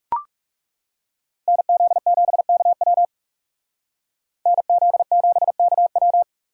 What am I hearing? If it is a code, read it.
Morse code, heard as a keyed tone, about 38 wpm, 700 Hz. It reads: NZ7KW NZ7KW